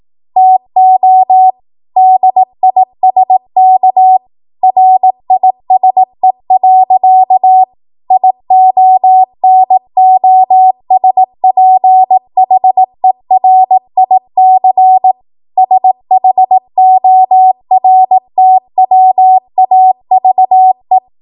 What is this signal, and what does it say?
Morse code, heard as a keyed tone, about 18 wpm, 750 Hz. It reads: TO DISK RISE. IONOSPHERIC SHORTWAVE
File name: Morse / 18WPM